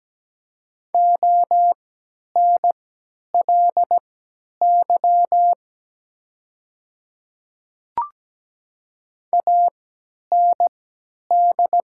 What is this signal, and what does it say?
Morse code, heard as a keyed tone, about 17 wpm, spaced as for 10 wpm, 700 Hz. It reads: ONLY AND
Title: 3x Character Spacing - Top 200 Words - 17wpm